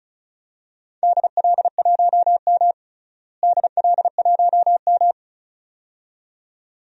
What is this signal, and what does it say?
Morse code, heard as a keyed tone, about 36 wpm, 700 Hz. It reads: DL1M DL1M
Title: DXCC-100 Call Signs - 4 Characters - Repeated 2x - 35wpm